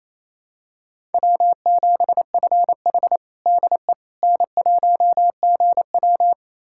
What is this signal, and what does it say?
Morse code, heard as a keyed tone, about 28 wpm, 700 Hz. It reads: W7FH DE N1GW
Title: USA Callsign DE Callsign - 4 Characters - 28wpm